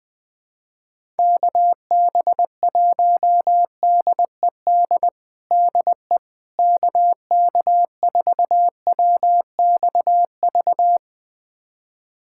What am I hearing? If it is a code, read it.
Morse code, heard as a keyed tone, about 20 wpm, 700 Hz. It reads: KB1DED DE KK4WXV
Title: USA Callsign DE Callsign - 6 Characters - 20wpm